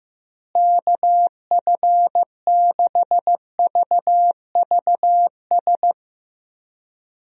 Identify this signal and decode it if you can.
Morse code, heard as a keyed tone, about 15 wpm, 700 Hz. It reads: KF6VVS
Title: USA Call Signs - 6 Characters - Repeated 3x - 15wpm